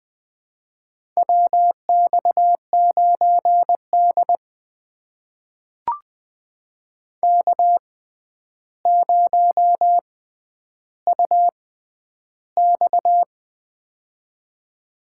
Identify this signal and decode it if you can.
Morse code, heard as a keyed tone, about 20 wpm, spaced as for 10 wpm, 700 Hz. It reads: WX9D K0UX
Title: USA Call Signs - 4 Characters - 6x Character Spacing - 20wpm